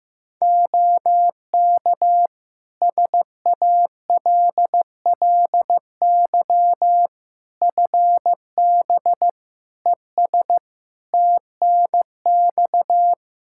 Morse code, heard as a keyed tone, about 16 wpm, 700 Hz. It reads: OK SALLY FB ES TNX
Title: Practice QSO - Exchange 2 - Info - 15wpm